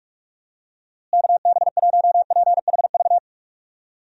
Morse code, heard as a keyed tone, about 46 wpm, 700 Hz. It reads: KB1PHV
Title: USA Call Signs - 6 Characters - Repeated 2x - 45wpm